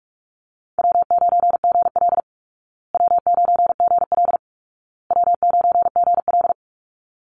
Morse code, heard as a keyed tone, about 44 wpm, 700 Hz. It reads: W9GL W9GL W9GL
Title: Call Signs Repeated 2x - 45wpm